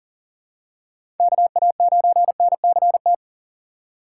Morse code, heard as a keyed tone, about 40 wpm, 700 Hz. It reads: KA9NCT
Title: USA Call Signs - 6 Characters - 6x Character Spacing - 40wpm